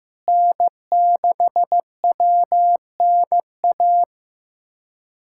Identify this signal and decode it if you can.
Morse code, heard as a keyed tone, about 15 wpm, 700 Hz. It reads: N6WNA
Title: USA Callsign DE Callsign - 5 Characters - 15wpm